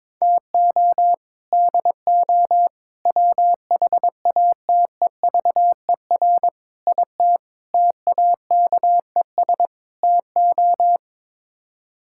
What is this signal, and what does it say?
Morse code, heard as a keyed tone, about 22 wpm, 700 Hz. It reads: TO DO WHATEVER IT TAKES TO
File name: practice / 22wpm / Sets of 6 Words - 22wpm